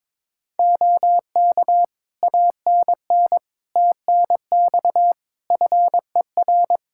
Morse code, heard as a keyed tone, about 23 wpm, 700 Hz. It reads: OK ANN TNX FER
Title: K1USN SST QSO Practice 22wpm